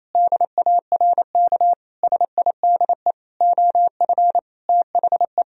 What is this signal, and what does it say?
Morse code, heard as a keyed tone, about 28 wpm, 700 Hz. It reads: DARK SIDE OF THE
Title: Sentences from Top 1000 Words - Review Part 3 - 28wpm